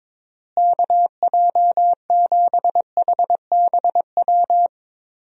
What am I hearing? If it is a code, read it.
Morse code, heard as a keyed tone, about 22 wpm, 700 Hz. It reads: KJ7HBW
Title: USA Call Signs - 6 Characters - Repeated 3x - 22wpm